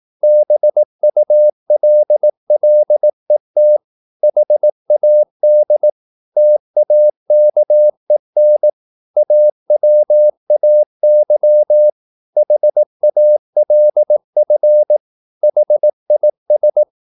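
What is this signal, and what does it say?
Morse code, heard as a keyed tone, about 18 wpm, 600 Hz. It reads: BULLET HAD TAKEN AWAY HALF HIS
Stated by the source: Morse practice at 18 wpm, English text, from CW med Gnister.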